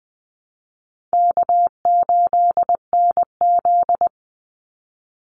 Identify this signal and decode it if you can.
Morse code, heard as a keyed tone, about 20 wpm, 700 Hz. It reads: K8NZ
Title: Call Signs Repeated 3x - 20wpm